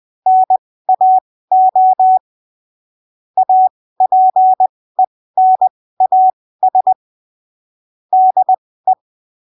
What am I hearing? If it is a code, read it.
Morse code, heard as a keyed tone, about 20 wpm, spaced as for 15 wpm, 750 Hz. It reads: NAO APENAS DE
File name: 20ppm